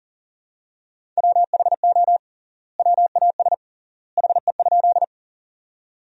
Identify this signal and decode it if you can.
Morse code, heard as a keyed tone, about 38 wpm, 700 Hz. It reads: WHO WAS HE?
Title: 3x Word-Spacing - Sentences from Top 100 Words - 40wpm